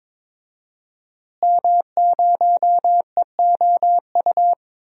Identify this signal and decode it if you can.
Morse code, heard as a keyed tone, about 21 wpm, 700 Hz. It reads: M0EOU
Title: DXCC-100 Call Signs - 5 Characters - Repeated 3x - 22wpm